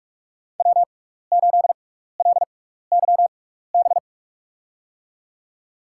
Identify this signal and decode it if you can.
Morse code, heard as a keyed tone, about 44 wpm, spaced as for 15 wpm, 700 Hz. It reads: W8LYB
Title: USA Call Signs - 5 Characters - 6x Character Spacing - 45wpm